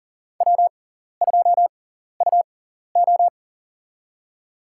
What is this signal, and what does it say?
Morse code, heard as a keyed tone, about 39 wpm, spaced as for 14 wpm, 700 Hz. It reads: W2UO
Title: USA Call Signs - 4 Characters - 6x Character Spacing - 40wpm